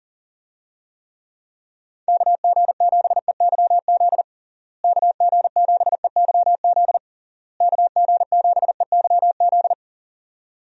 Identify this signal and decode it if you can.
Morse code, heard as a keyed tone, about 40 wpm, 700 Hz. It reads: KG7EYZ KG7EYZ KG7EYZ
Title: USA Call Signs - 6 Characters - Repeated 3x - 40wpm